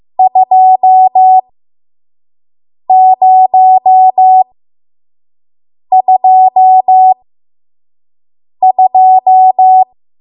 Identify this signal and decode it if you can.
Morse code, heard as a keyed tone, about 15 wpm, spaced as for 5 wpm, 750 Hz. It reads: 2022
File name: Morse / 05WPM